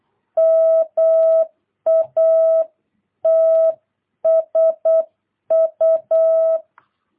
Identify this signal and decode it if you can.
Morse code, heard as a keyed tone, about 8 wpm, 660 Hz. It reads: MATSU
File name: jh8cbh / 2022cw